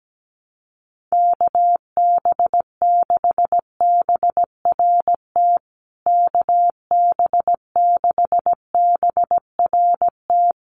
Morse code, heard as a keyed tone, about 17 wpm, 700 Hz. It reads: KB6BRT KB6BRT
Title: USA Call Signs Repeated 3x 17wpm